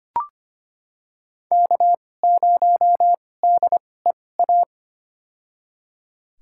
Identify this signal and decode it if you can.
Morse code, heard as a keyed tone, about 25 wpm, spaced as for 18 wpm, 700 Hz. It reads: K0DEA
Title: USA Call Signs - 5 Characters - 2x Character Spacing - 25wpm